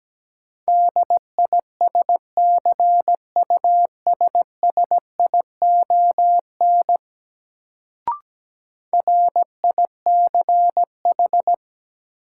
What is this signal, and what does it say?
Morse code, heard as a keyed tone, about 17 wpm, 700 Hz. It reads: DISCUSSION RICH
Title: Sentences from Top 1100 Words - New Word Drill - 17wpm